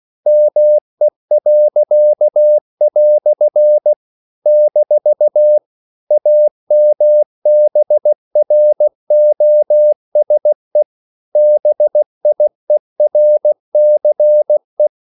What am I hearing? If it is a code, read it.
Morse code, heard as a keyed tone, about 16 wpm, 600 Hz. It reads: ME." - AMBROSE BIERCE
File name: CW / Practice / EnglishQuotations0216